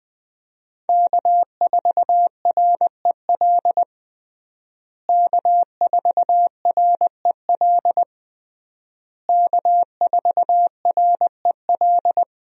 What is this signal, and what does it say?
Morse code, heard as a keyed tone, about 21 wpm, 700 Hz. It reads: K4REL K4REL K4REL
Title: USA Call Signs - 5 Characters - Repeated 3x - 20wpm